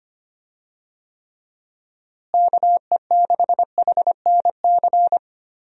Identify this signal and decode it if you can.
Morse code, heard as a keyed tone, about 25 wpm, 700 Hz. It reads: KE6HNC
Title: USA Call Signs - 6 Characters - Repeated 2x - 25wpm